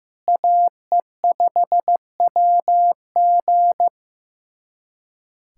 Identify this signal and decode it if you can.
Morse code, heard as a keyed tone, about 15 wpm, 700 Hz. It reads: AE5WG
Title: USA Callsign DE Callsign - 5 Characters - 15wpm